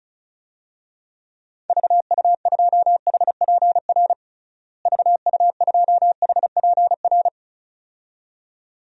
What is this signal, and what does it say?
Morse code, heard as a keyed tone, about 36 wpm, 700 Hz. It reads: VU2HPR VU2HPR
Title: DXCC-100 Call Signs - 6 Characters - Repeated 2x - 35wpm